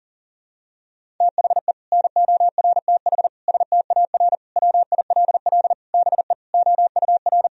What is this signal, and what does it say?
Morse code, heard as a keyed tone, about 40 wpm, 700 Hz. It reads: THE NORTH STAR WILL BE OUR
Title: Sentences from Top 1000 Words - Review Part 3 - 40wpm